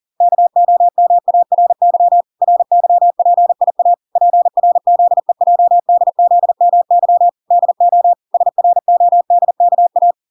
Morse code, heard as a keyed tone, about 40 wpm, 700 Hz. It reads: KOMARY RYPIA PRZEJDZMY DO SRODKA